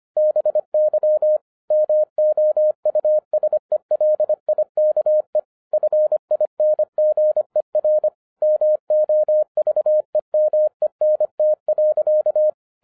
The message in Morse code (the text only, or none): BY MOUSELIKE FINGER MOVEMENT.